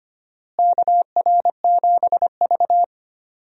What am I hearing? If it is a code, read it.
Morse code, heard as a keyed tone, about 25 wpm, 700 Hz. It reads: KR7V
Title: USA Call Signs - 4 Characters - Repeated 2x - 25wpm